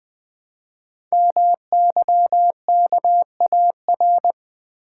Morse code, heard as a keyed tone, about 20 wpm, 700 Hz. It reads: MYKAR
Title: Code Groups - 5 Letters - 20wpm